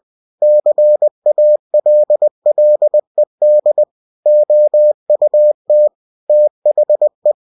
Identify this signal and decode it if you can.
Morse code, heard as a keyed tone, about 20 wpm, 600 Hz. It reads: CALLED OUT THE
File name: cw-books / winnie-the-pooh / winnie-20wpm-page-0009